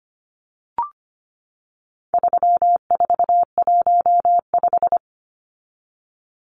Morse code, heard as a keyed tone, about 25 wpm, 700 Hz. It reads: 3415
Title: Numbers - 4-Digit 25wpm